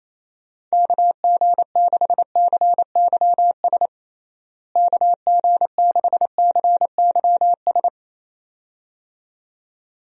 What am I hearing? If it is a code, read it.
Morse code, heard as a keyed tone, about 28 wpm, 700 Hz. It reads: KG6CYS KG6CYS
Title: USA Call Signs - 6 Characters - Repeated 2x - 28wpm